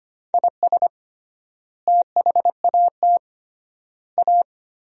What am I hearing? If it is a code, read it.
Morse code, heard as a keyed tone, about 25 wpm, 700 Hz. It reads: IS THAT A
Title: 3x Word-Spacing - Sentences from Top 500 Words - 25wpm